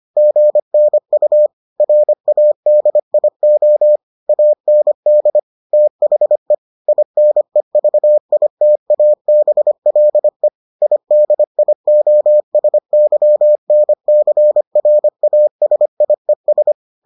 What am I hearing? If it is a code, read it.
Morse code, heard as a keyed tone, about 25 wpm, 600 Hz. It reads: GNU RADIO AND THE INEVITABLE IDIOSYNCRASIES